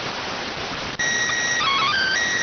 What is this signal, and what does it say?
SSTV (slow-scan television), PD 290.